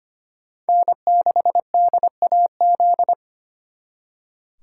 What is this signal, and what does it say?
Morse code, heard as a keyed tone, about 33 wpm, 700 Hz. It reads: N6DAZ